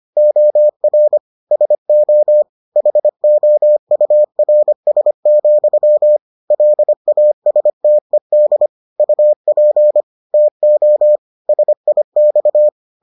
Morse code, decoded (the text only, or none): OR SO HOURS, LASTED UP TO SIX